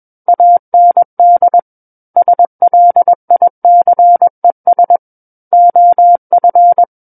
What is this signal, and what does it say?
Morse code, heard as a keyed tone, about 21 wpm, 700 Hz. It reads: AND SLICES OF
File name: Book One / 12-Chapter 12 - 21 WPM